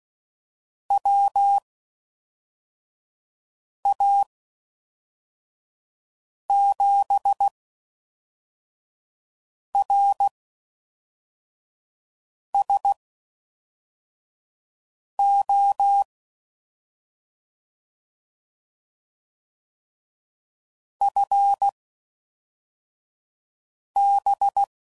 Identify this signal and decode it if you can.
Morse code, heard as a keyed tone, about 16 wpm, spaced as for 4 wpm, 800 Hz. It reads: WA7RSO FB